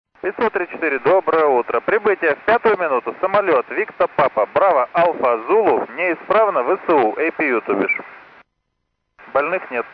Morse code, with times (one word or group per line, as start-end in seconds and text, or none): none